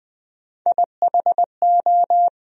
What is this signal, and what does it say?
Morse code, heard as a keyed tone, about 20 wpm, 700 Hz. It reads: IHO
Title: USA Call Sign Postfixes - 3 Letters 20wpm